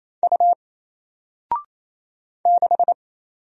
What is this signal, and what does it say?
Morse code, heard as a keyed tone, about 27 wpm, 700 Hz. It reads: U 6
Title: Single Letter-Number Alt 28wpm